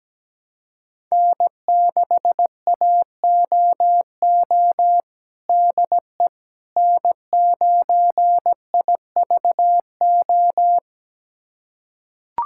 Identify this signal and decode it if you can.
Morse code, heard as a keyed tone, about 17 wpm, 700 Hz. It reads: N6AOO DE N9IVO
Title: USA Callsign DE Callsign - 5 Characters - 17wpm